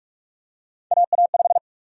Morse code, heard as a keyed tone, about 45 wpm, 700 Hz. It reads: AA5
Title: USA Call Sign Prefixes - Repeated 2x - 45wpm